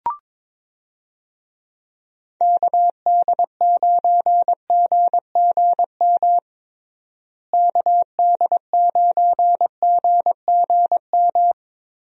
Morse code, heard as a keyed tone, about 22 wpm, 700 Hz. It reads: KD9GGM KD9GGM
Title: USA Call Signs - 6 Characters - Repeated 3x - 22wpm